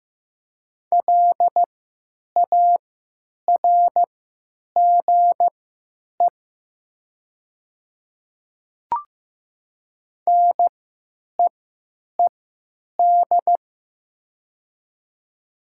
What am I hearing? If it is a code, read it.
Morse code, heard as a keyed tone, about 15 wpm, spaced as for 9 wpm, 700 Hz. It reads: LARGE NEED
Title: 3x Character Spacing - Top 200 Words - 15wpm